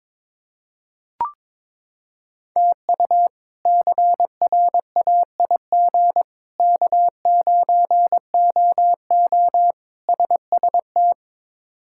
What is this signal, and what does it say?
Morse code, heard as a keyed tone, about 22 wpm, 700 Hz. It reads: TU CRAIG K9OO SST
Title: K1USN SST QSO Practice - 22wpm